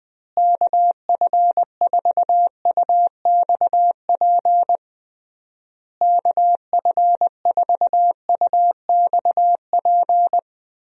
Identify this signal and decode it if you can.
Morse code, heard as a keyed tone, about 19 wpm, 700 Hz. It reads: KF4UXP KF4UXP
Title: USA Call Signs - 6 Characters - Repeated 3x - 20wpm